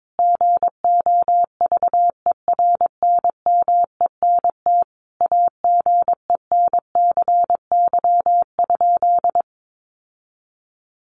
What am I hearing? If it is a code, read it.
Morse code, heard as a keyed tone, about 22 wpm, 700 Hz. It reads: GOVERNMENT AGENCY?
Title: Sentences from Top 1300 Words 22wpm